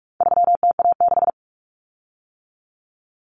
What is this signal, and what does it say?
Morse code, heard as a keyed tone, about 44 wpm, 700 Hz. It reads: 3TA6
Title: Common Cut Numbers - 4-Digit 45wpm